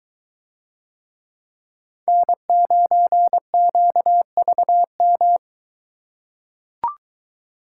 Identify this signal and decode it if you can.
Morse code, heard as a keyed tone, about 23 wpm, 700 Hz. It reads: N9QVM